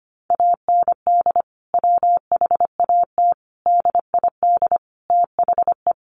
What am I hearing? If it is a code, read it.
Morse code, heard as a keyed tone, about 25 wpm, 700 Hz. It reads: AND WHAT DID THE
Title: Sentences from Top 1500 Words 25wpm